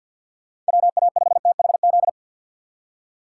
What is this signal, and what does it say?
Morse code, heard as a keyed tone, about 64 wpm, 700 Hz. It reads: WA5THZ